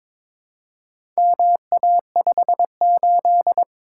Morse code, heard as a keyed tone, about 22 wpm, 690 Hz. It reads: MA58
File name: practice / 22wpm / Four Letter-Number Combinations - 22wpm